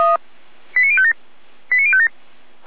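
Five-tone selective call (ZVEI), heard as quick runs of five tones, tones of about 70 ms. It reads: 89057 89057
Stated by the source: ZVEI five-tone selective call received by SDR.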